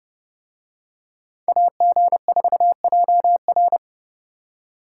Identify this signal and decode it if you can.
Morse code, heard as a keyed tone, about 30 wpm, 700 Hz. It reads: AG4JR